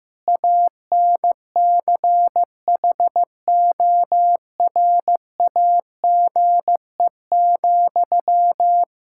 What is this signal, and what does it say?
Morse code, heard as a keyed tone, about 15 wpm, 700 Hz. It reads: ANCHORAGE,
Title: K1USN SST QSO Practice 15wpm